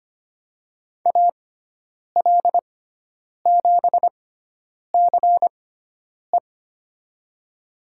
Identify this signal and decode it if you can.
Morse code, heard as a keyed tone, about 25 wpm, spaced as for 9 wpm, 700 Hz. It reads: AL7CE
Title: USA Call Signs - 6x Character Spacing - 25wpm